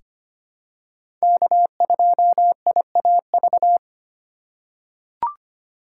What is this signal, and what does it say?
Morse code, heard as a keyed tone, about 25 wpm, 700 Hz. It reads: K2IAV
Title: USA Call Signs - 5 Characters - 3x Character Spacing - 25wpm